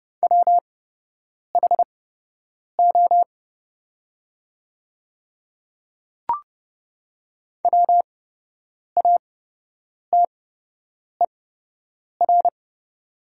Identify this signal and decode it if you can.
Morse code, heard as a keyed tone, about 30 wpm, spaced as for 8 wpm, 700 Hz. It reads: WHO WATER
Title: 8x Character Spacing - Top 100 Words - 30wpm